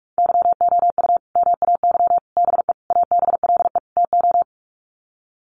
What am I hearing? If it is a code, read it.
Morse code, heard as a keyed tone, about 44 wpm, 700 Hz. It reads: YOU MAY BE ABLE TO
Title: Sets of 5 Words Encore 45wpm